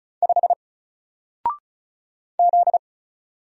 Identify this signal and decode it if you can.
Morse code, heard as a keyed tone, about 34 wpm, 700 Hz. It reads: 5 Z
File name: Single-Letter-Number-35wpm